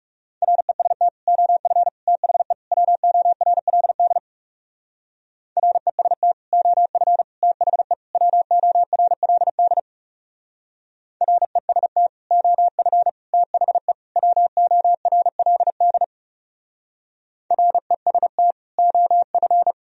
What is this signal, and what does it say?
Morse code, heard as a keyed tone, about 36 wpm, 690 Hz. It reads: REST OF THE WORLD REST OF THE WORLD REST OF THE WORLD REST OF
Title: Speed-Racing - Sets of 4 Words 30wpm